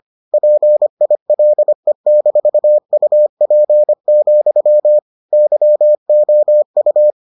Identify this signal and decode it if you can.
Morse code, heard as a keyed tone, about 25 wpm, 600 Hz. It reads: PILE-UP, YOU